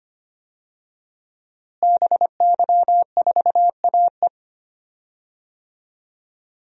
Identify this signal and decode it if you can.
Morse code, heard as a keyed tone, about 25 wpm, 700 Hz. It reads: BY4AE
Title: DXCC-100 Call Signs - 5 Characters - Repeated 2x - 25wpm